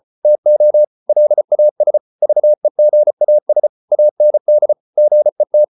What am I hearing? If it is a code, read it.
Morse code, heard as a keyed tone, about 34 wpm, 600 Hz. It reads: TO LAS VEGAS AND GET